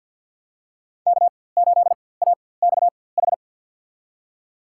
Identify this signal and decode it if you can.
Morse code, heard as a keyed tone, about 51 wpm, spaced as for 23 wpm, 700 Hz. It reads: K8AXH
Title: USA Call Signs - 5 Characters - 4x Character Spacing - 50wpm